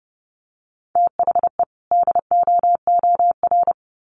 Morse code, heard as a keyed tone, about 30 wpm, 700 Hz. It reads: THE DOOR
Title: Sets of 2 Words 30wpm